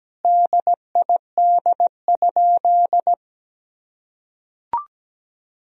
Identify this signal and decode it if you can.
Morse code, heard as a keyed tone, about 17 wpm, 700 Hz. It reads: DID?
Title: Sentences from Top 100 Words and Abbreviations 17wpm